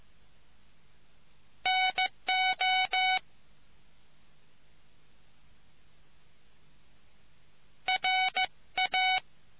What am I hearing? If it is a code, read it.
Morse code, heard as a keyed tone, about 15 wpm, 750 Hz. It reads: NO RA